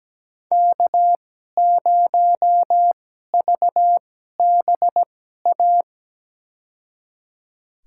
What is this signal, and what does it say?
Morse code, heard as a keyed tone, about 17 wpm, spaced as for 12 wpm, 700 Hz. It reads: K0VBA
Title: USA Call Signs - 5 Characters - 2x Character Spacing - 17wpm